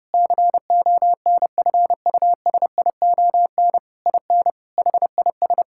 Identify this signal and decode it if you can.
Morse code, heard as a keyed tone, about 30 wpm, 700 Hz. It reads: CONFUSION IN HIS
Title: Sentences from Top 1600 Words 30wpm